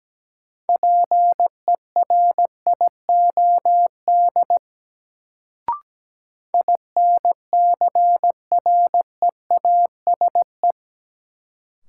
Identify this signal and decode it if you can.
Morse code, heard as a keyed tone, about 17 wpm, 700 Hz. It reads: PERIOD INCREASE
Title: Sets of 5 Words - New Word Drill - 17wpm